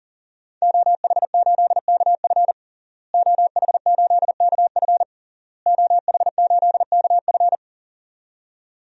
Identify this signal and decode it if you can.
Morse code, heard as a keyed tone, about 40 wpm, 700 Hz. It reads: OH8KF OH8KF OH8KF
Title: DXCC-100 Call Signs - 5 Characters - Repeated 3x - 40wpm